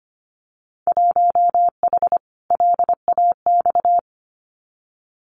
Morse code, heard as a keyed tone, about 25 wpm, 700 Hz. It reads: 1H LAX
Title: Winter Field Day Exchanges 25wpm